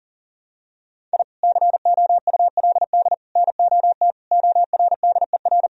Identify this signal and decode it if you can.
Morse code, heard as a keyed tone, about 40 wpm, 700 Hz. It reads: I COULD NOT ORDER
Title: Sentences from Top 1000 Words - Review Part 2 - 40wpm